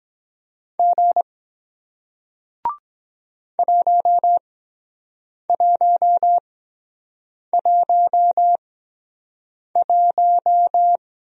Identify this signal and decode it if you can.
Morse code, heard as a keyed tone, about 23 wpm, spaced as for 7 wpm, 700 Hz. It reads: G 1111